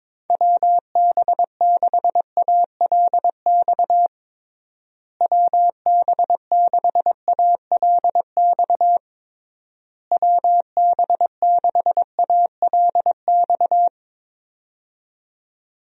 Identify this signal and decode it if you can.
Morse code, heard as a keyed tone, about 22 wpm, 700 Hz. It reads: WB6ALX WB6ALX WB6ALX
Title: USA Call Signs - 6 Characters - Repeated 3x - 22wpm